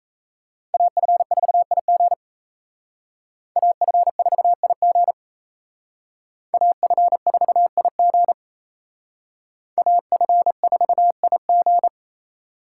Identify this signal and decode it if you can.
Morse code, heard as a keyed tone, about 36 wpm, 700 Hz. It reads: AF4IG AF4IG AF4IG AF4IG